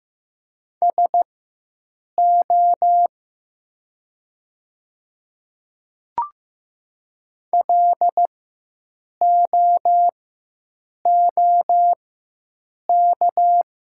Morse code, heard as a keyed tone, about 15 wpm, spaced as for 7 wpm, 700 Hz. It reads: SO LOOK
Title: 4x Character Spacing - Top 200 Words - 15wpm